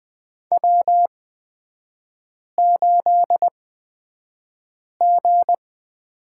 Morse code, heard as a keyed tone, about 20 wpm, spaced as for 5 wpm, 690 Hz. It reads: W8G
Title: 229 Lesson 37e - Letter x - Three Characters 20wpm